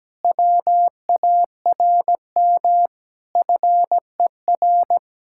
Morse code, heard as a keyed tone, about 17 wpm, 700 Hz. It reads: WARM FER
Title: Speed-Racing Common QSO Phrases - 17wpm